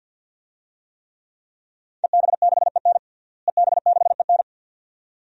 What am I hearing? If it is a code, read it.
Morse code, heard as a keyed tone, about 52 wpm, 700 Hz. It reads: EB6EN EB6EN
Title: DXCC-100 Call Signs - 5 Characters - Repeated 2x - 50wpm